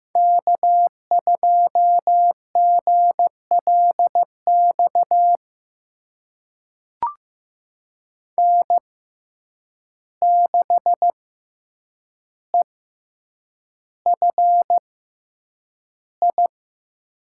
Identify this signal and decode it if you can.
Morse code, heard as a keyed tone, about 15 wpm, spaced as for 8 wpm, 700 Hz. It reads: K2GLX N6EFI
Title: USA Call Signs - 5 Characters - 6x Character Spacing - 15wpm